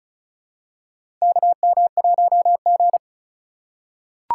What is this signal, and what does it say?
Morse code, heard as a keyed tone, about 36 wpm, 700 Hz. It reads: KM1G
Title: USA Call Signs - 4 Characters - 2x Character Spacing - 35wpm